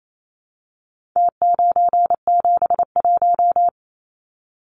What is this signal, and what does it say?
Morse code, heard as a keyed tone, about 28 wpm, 700 Hz. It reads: T971
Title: Numbers - Contest Style - 4-Digit 28wpm